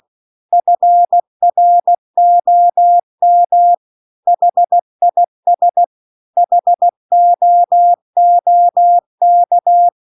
Morse code, heard as a keyed tone, about 16 wpm, 700 Hz. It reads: FROM HIS HOOK